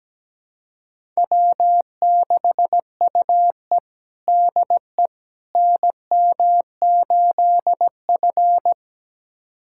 Morse code, heard as a keyed tone, about 17 wpm, 700 Hz. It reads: W6UE DE NM8F
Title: USA Callsign DE Callsign - 4 Characters - 17wpm